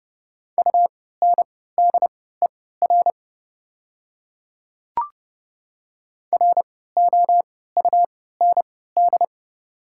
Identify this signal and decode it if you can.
Morse code, heard as a keyed tone, about 32 wpm, spaced as for 17 wpm, 690 Hz. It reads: UNDER ROUND
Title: 3x Character Spacing - Top 200 Words - 30wpm